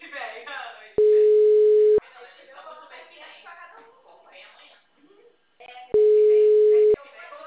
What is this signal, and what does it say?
Ringback tone (Europe, 425 Hz), with speech.
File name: out- -2020